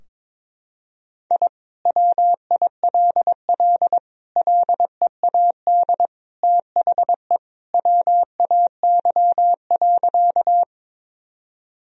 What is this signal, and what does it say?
Morse code, heard as a keyed tone, about 21 wpm, 700 Hz. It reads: I WILL LEAD THE WAY.
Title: Sentences from Top 1000 Words - Review Part 2 - 22wpm